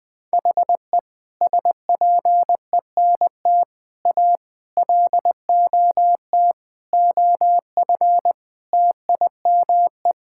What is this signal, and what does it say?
Morse code, heard as a keyed tone, about 20 wpm, 700 Hz. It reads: HE SPENT A LOT OF TIME
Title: Sentences with 5-Word Phrases - 20wpm